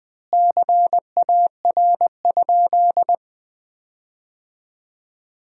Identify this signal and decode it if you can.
Morse code, heard as a keyed tone, about 20 wpm, 700 Hz. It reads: CAR?
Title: 2x Word-Spacing - Sentences from Top 300 Words - 20wpm